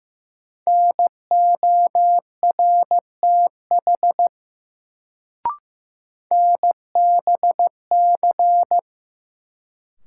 Morse code, heard as a keyed tone, about 15 wpm, 700 Hz. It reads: NORTH NBC
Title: Sets of 5 Words Encore - New Word Drill - 15wpm